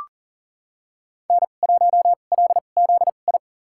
Morse code, heard as a keyed tone, about 40 wpm, spaced as for 29 wpm, 700 Hz. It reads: N1LZI
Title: USA Call Signs - 2x Character Spacing - 40wpm